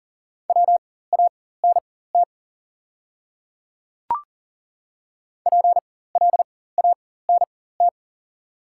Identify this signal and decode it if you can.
Morse code, heard as a keyed tone, about 40 wpm, spaced as for 19 wpm, 700 Hz. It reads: WANT PLANT
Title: 4x Character Spacing - Top 300 Words - 40wpm